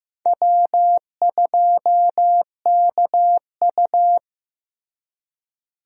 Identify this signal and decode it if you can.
Morse code, heard as a keyed tone, about 15 wpm, 700 Hz. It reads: W2KU
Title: USA Call Signs - 4 Characters - Repeated 3x - 15wpm